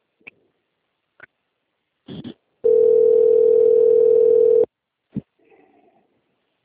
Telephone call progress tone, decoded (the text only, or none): ringback tone